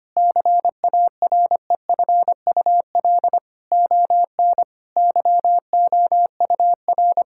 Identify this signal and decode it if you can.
Morse code, heard as a keyed tone, about 25 wpm, 700 Hz. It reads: CAREFUL ON YOUR
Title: Sentences from Top 1600 Words 25wpm